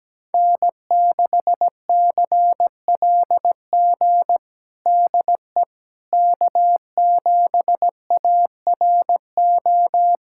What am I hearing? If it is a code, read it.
Morse code, heard as a keyed tone, about 17 wpm, 700 Hz. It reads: N6CLG DE K7ARO
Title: USA Callsign DE Callsign - 5 Characters - 17wpm